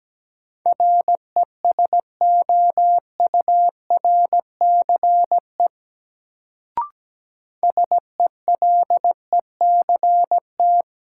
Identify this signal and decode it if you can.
Morse code, heard as a keyed tone, about 17 wpm, 700 Hz. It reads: RESOURCE SELECT
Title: Sentences from Top 1300 Words - New Word Drill Alt - 17wpm